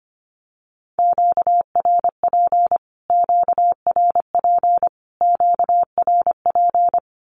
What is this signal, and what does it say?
Morse code, heard as a keyed tone, about 25 wpm, 700 Hz. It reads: QRP QRP QRP
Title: Q-Codes Repeated 3x 25wpm